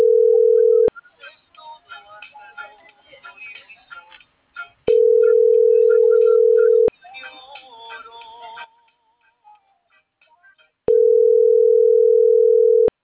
Ringback tone (North America), 440 + 480 Hz.